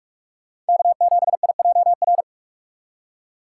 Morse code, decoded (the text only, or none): K7IJR